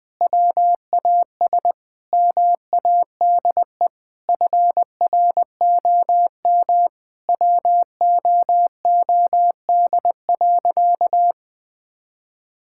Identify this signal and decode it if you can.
Morse code, heard as a keyed tone, about 21 wpm, 700 Hz. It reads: WAS MADE FROM WOOD.
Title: Sentences from Top 1000 Words - Review Part 6 - 20wpm